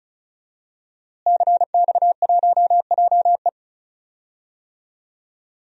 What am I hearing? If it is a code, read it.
Morse code, heard as a keyed tone, about 36 wpm, 700 Hz. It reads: CX1JE
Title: DXCC-100 Call Signs - 5 Characters - Repeated 3x - 35wpm